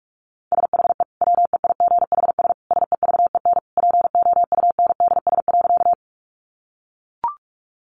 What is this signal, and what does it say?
Morse code, heard as a keyed tone, about 44 wpm, 700 Hz. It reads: SHE WEIGHS SEVEN POUNDS.
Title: Easy Phrases 45wpm